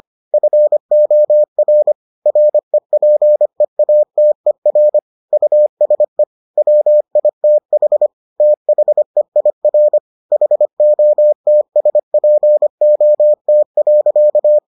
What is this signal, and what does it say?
Morse code, heard as a keyed tone, about 25 wpm, 600 Hz. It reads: FOR REPEATER USE WITH THEIR HOTSPOT.